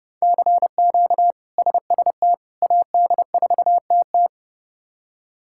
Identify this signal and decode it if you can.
Morse code, heard as a keyed tone, about 31 wpm, 700 Hz. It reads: CQ SST AD4TT
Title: K1USN SST QSO Practice - 30wpm